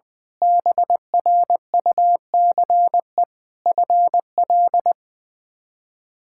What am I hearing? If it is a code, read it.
Morse code, heard as a keyed tone, about 20 wpm, 700 Hz. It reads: BRUCE FL
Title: K1USN SST QSO Practice - 20wpm